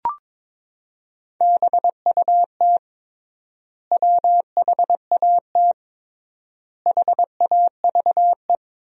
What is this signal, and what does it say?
Morse code, heard as a keyed tone, about 21 wpm, 700 Hz. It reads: BUT WHAT HAVE
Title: 3x Word-Spacing - Sentences from Top 100 Words - 22wpm